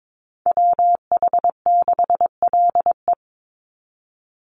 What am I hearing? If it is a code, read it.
Morse code, heard as a keyed tone, about 22 wpm, 700 Hz. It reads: WH6LE
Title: NA SOTA Activator Callsigns 22wpm